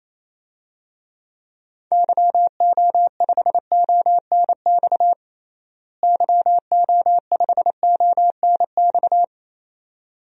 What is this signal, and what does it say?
Morse code, heard as a keyed tone, about 28 wpm, 700 Hz. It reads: YO5ONX YO5ONX
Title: DXCC-100 Call Signs - 6 Characters - Repeated 2x - 28wpm